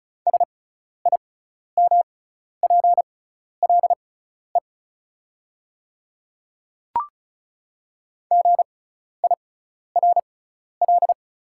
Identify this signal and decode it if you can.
Morse code, heard as a keyed tone, about 36 wpm, spaced as for 12 wpm, 690 Hz. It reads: SIMPLE GIRL
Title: 6x Character Spacing - Top 500 Words - 35wpm